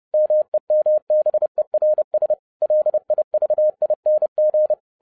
Morse code, heard as a keyed tone, about 30 wpm, 610 Hz. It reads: MEMBERS LIVING